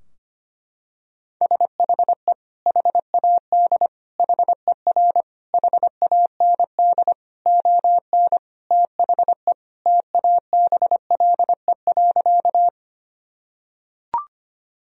Morse code, heard as a keyed tone, about 25 wpm, 700 Hz. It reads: SHE HAD HER HAND ON THE TABLE.
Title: Sentences from Top 1000 Words - Review Part 2 - 25wpm